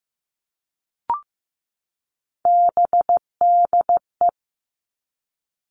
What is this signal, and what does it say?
Morse code, heard as a keyed tone, about 15 wpm, 700 Hz. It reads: BDE
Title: Three Challenging Letter-Number Combinations 15wpm